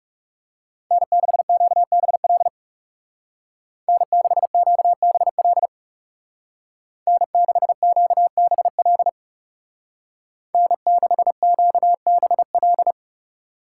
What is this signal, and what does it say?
Morse code, heard as a keyed tone, about 35 wpm, 700 Hz. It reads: N6QBL N6QBL N6QBL N6QBL
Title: Speed Racing - USA 1x3 Call Signs 30wpm